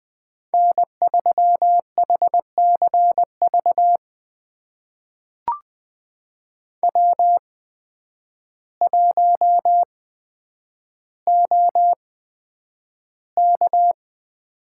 Morse code, heard as a keyed tone, about 20 wpm, spaced as for 9 wpm, 700 Hz. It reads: N3HCV W1OK
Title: USA Call Signs - 5 Characters - 8x Character Spacing - 20wpm